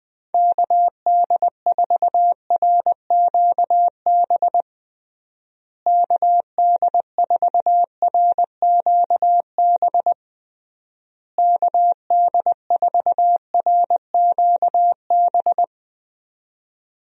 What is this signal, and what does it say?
Morse code, heard as a keyed tone, about 20 wpm, 700 Hz. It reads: KD4RQB KD4RQB KD4RQB
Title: USA Call Signs - 6 Characters - Repeated 3x - 20wpm